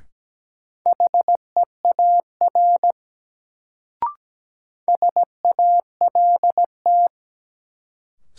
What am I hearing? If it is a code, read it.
Morse code, heard as a keyed tone, about 17 wpm, 700 Hz. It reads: HEAR SALT
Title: Sets of 5 Words - New Word Drill - 17wpm